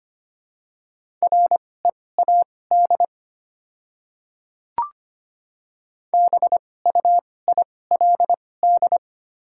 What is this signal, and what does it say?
Morse code, heard as a keyed tone, about 25 wpm, spaced as for 18 wpm, 690 Hz. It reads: READ BUILD
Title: 2x Character Spacing - Top 500 Words - 25wpm